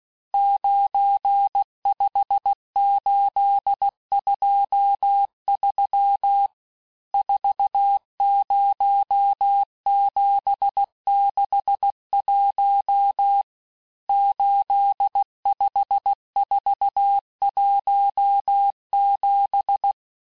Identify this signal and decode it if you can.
Morse code, heard as a keyed tone, about 16 wpm, 800 Hz. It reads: 95823 40761 85417
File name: cours-du-lundi files / cours-1592--teste-chiffre-vitesse-15mpm